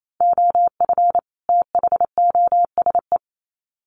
Morse code, heard as a keyed tone, about 28 wpm, 700 Hz. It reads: OF THOSE
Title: Sets of 2 Words 28wpm